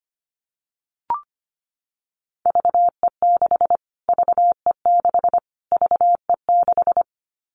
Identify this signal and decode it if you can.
Morse code, heard as a keyed tone, about 25 wpm, 700 Hz. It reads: VE6 VE6 VE6
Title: Callsign Prefixes for Top 25 DXCC - Repeated 3x - 25wpm